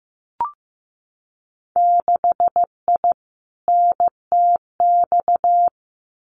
Morse code, heard as a keyed tone, about 15 wpm, 700 Hz. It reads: 6I NTX
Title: Winter Field Day Exchanges 15wpm